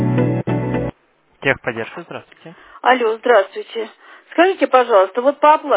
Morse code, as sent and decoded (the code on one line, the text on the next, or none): none